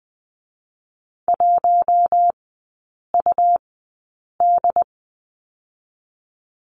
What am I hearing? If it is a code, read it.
Morse code, heard as a keyed tone, about 20 wpm, 700 Hz. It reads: 1 U D
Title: 107 Lesson 18e - Letter m - Three Characters 20wpm